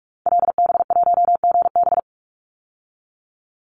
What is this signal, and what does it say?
Morse code, heard as a keyed tone, about 44 wpm, 700 Hz. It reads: LB1GB
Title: Call Signs Repeated 3x - 45wpm